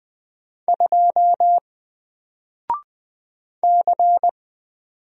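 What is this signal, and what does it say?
Morse code, heard as a keyed tone, about 20 wpm, 700 Hz. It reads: 2 C
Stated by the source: Single letter and number Morse drill at 20 wpm, from N4LCD.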